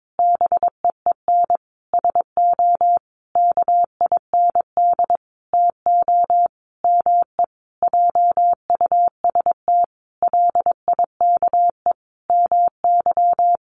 Morse code, heard as a keyed tone, about 22 wpm, 700 Hz. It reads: BEEN SO KIND TO ME JUST LIKE MY
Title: Sentences from Top 1200 Words 22wpm